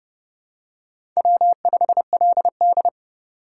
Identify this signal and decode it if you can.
Morse code, heard as a keyed tone, about 30 wpm, 700 Hz. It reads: W5LD
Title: USA Call Signs - 4 Characters - 2x Character Spacing - 30wpm